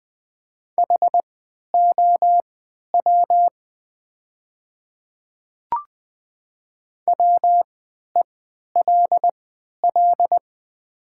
Morse code, heard as a keyed tone, about 20 wpm, spaced as for 11 wpm, 700 Hz. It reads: HOW WELL
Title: 3x Character Spacing - Top 200 Words - 20wpm